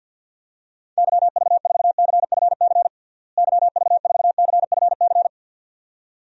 Morse code, heard as a keyed tone, about 51 wpm, 700 Hz. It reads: YV4CFC YV4CFC
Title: DXCC-100 Call Signs - 6 Characters - Repeated 2x - 50wpm